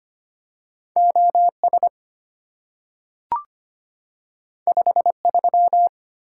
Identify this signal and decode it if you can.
Morse code, heard as a keyed tone, about 25 wpm, 700 Hz. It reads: OS 53
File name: Two Letter-Number Combinations - 25wpm